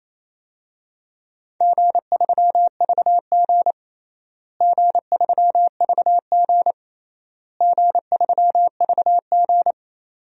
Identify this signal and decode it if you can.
Morse code, heard as a keyed tone, about 28 wpm, 700 Hz. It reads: G3VG G3VG G3VG
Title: DXCC-100 Call Signs - 4 Characters - Repeated 3x - 28wpm